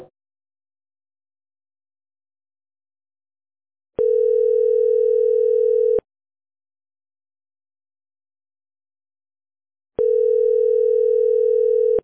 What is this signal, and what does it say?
Ringback tone (North America), 440 + 480 Hz.